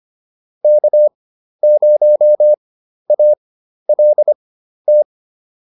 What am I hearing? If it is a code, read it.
Morse code, heard as a keyed tone, about 25 wpm, spaced as for 12 wpm, 600 Hz. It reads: K0ALT